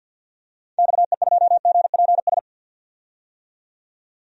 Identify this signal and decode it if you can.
Morse code, heard as a keyed tone, about 51 wpm, 700 Hz. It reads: XE2GPS
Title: DXCC-100 Call Signs - 6 Characters - 50wpm